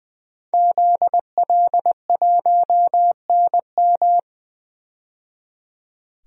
Morse code, heard as a keyed tone, about 20 wpm, 700 Hz. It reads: ZL1NM
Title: DXCC-100 Call Signs - 5 Characters - 20wpm